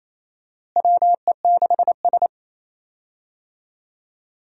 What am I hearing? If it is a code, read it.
Morse code, heard as a keyed tone, about 27 wpm, 700 Hz. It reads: WE6S
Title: USA Call Signs - 4 Characters - Repeated 3x - 28wpm